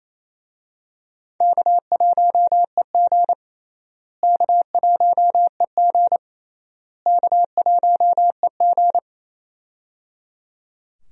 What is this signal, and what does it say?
Morse code, heard as a keyed tone, about 27 wpm, 700 Hz. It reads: K1EG K1EG K1EG
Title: USA Call Signs - 4 Characters - Repeated 3x - 28wpm